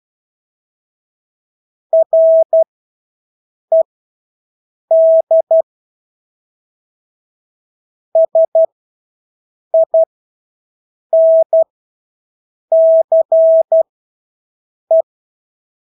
Morse code, heard as a keyed tone, about 12 wpm, spaced as for 6 wpm, 650 Hz. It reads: RED SINCE
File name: Engelske ord 12-6 wpm